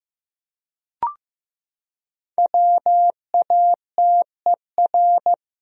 Morse code, heard as a keyed tone, about 15 wpm, 700 Hz. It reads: WATER
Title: Top 100 Words Repeated 15wpm